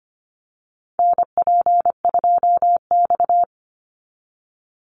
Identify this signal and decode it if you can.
Morse code, heard as a keyed tone, about 25 wpm, 700 Hz. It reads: NP2X
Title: Call Signs - 25wpm